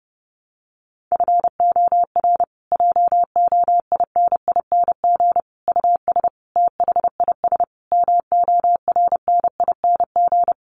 Morse code, heard as a keyed tone, about 30 wpm, 700 Hz. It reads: FOR JOINING US THIS MORNING
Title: Sets of 5 Words Encore 30wpm